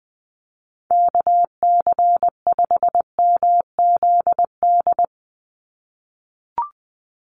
Morse code, heard as a keyed tone, about 20 wpm, 700 Hz. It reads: KC5MZD
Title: USA Call Signs 20wpm